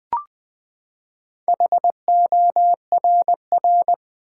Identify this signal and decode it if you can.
Morse code, heard as a keyed tone, about 20 wpm, 700 Hz. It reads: HORR
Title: Build A Word - 4 Letters At A Time - 8-Letter Words - 20wpm